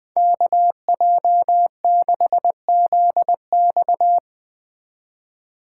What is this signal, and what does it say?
Morse code, heard as a keyed tone, about 20 wpm, 700 Hz. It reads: KJ6ZX